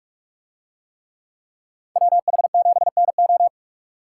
Morse code, heard as a keyed tone, about 46 wpm, 700 Hz. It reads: WH7NO